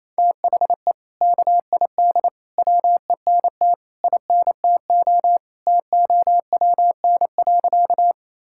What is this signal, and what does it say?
Morse code, heard as a keyed tone, about 28 wpm, 700 Hz. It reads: THE KID WENT INTO TOWN.
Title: Sentences from Top 1600 Words 28wpm